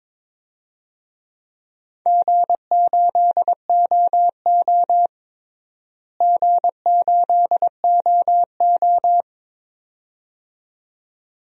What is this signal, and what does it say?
Morse code, heard as a keyed tone, about 22 wpm, 700 Hz. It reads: G8OO G8OO
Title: DXCC-100 Call Signs - 4 Characters - Repeated 2x - 22wpm